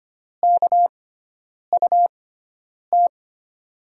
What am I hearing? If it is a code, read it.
Morse code, heard as a keyed tone, about 25 wpm, spaced as for 9 wpm, 700 Hz. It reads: KUT
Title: USA Call Signs - 6 Characters - 6x Character Spacing - 25wpm